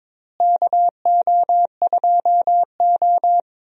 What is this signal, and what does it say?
Morse code, heard as a keyed tone, about 22 wpm, 700 Hz. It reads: KO2O
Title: USA Call Signs - 4 Characters - Repeated 2x - 22wpm